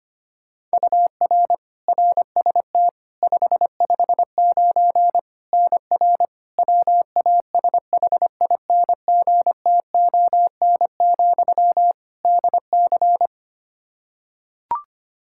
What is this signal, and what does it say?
Morse code, heard as a keyed tone, about 25 wpm, 700 Hz. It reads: UR RST 559 NR WASHINGTON, DC
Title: K1USN SST QSO Practice 25wpm